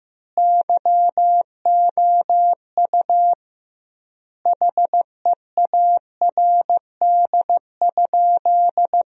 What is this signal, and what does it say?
Morse code, heard as a keyed tone, about 15 wpm, 700 Hz. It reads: YOU HEARD?
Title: 2x Word-Spacing - Sentences from Top 400 Words - 15wpm